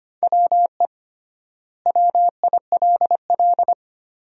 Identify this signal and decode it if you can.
Morse code, heard as a keyed tone, about 25 wpm, 700 Hz. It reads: WE WILL
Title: 3x Word-Spacing - Sentences from Top 200 Words - 25wpm